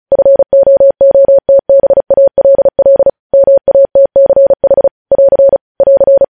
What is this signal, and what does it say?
Morse code, heard as a keyed tone, about 36 wpm, 550 Hz. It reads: FOOTBALL MATCH <AR> <AR>
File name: podcast / News headlines-1x-35x35